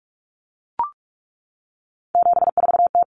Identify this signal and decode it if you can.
Morse code, heard as a keyed tone, about 44 wpm, 700 Hz. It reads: 74T
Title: Common Cut Numbers - 3-Digit 45wpm